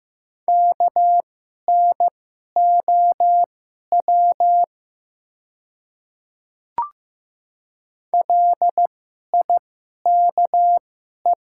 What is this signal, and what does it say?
Morse code, heard as a keyed tone, about 15 wpm, spaced as for 11 wpm, 700 Hz. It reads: KNOW LIKE
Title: 2x Character Spacing - Top 100 Words - 15wpm